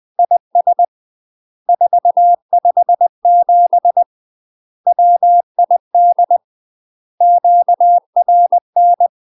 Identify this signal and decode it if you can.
Morse code, heard as a keyed tone, about 20 wpm, 700 Hz. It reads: IS 457 WID QRN
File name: UK MASTER QSOs TRAINING / 20WPM / UK MASTER QSOs 0010